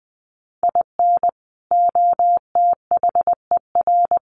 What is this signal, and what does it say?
Morse code, heard as a keyed tone, about 20 wpm, 700 Hz. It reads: IN OTHER
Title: Sets of 3 Words 20wpm